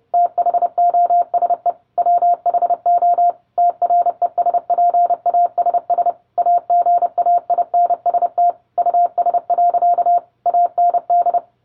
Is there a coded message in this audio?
Yes: Morse code from 0.1 to 11.4 s, heard as a keyed tone, about 30 wpm, 700 Hz. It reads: THOSE WHO TRESPASS AGAINST US. AND